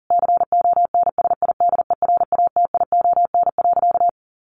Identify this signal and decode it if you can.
Morse code, heard as a keyed tone, about 37 wpm, 700 Hz. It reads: CONSIDERATION.
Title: Sentences from Top 1500 Words 40wpm